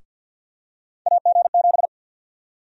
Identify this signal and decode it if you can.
Morse code, heard as a keyed tone, about 53 wpm, 700 Hz. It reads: AG7
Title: USA Call Sign Prefixes - 50wpm